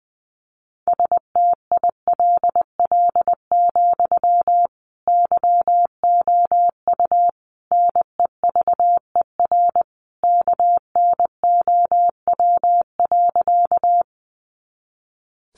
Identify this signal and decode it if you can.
Morse code, heard as a keyed tone, about 20 wpm, 700 Hz. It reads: STILL, YOU NEVER KNOW.
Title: Sentences from Top 300 Words 20wpm